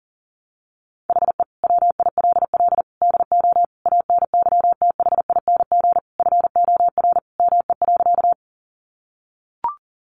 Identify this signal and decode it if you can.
Morse code, heard as a keyed tone, about 40 wpm, 700 Hz. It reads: HE WILL DO ANYTHING FOR ME.
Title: Sentences from Top 1000 Words 40wpm